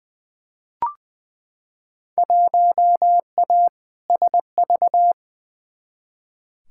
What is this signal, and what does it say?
Morse code, heard as a keyed tone, about 20 wpm, 700 Hz. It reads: 1A SV
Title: Field Day Exchanges - 20wpm